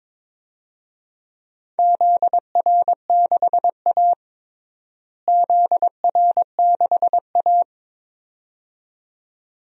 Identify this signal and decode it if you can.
Morse code, heard as a keyed tone, about 22 wpm, 700 Hz. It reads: ZR6A ZR6A
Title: DXCC-100 Call Signs - 4 Characters - Repeated 2x - 22wpm